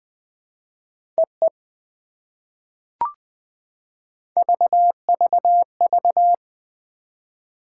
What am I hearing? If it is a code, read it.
Morse code, heard as a keyed tone, about 20 wpm, 700 Hz. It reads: EE VVV
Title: POTA QSO Practice - 20wpm